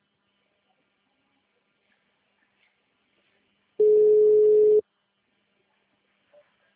Ringback tone (Europe, 425 Hz).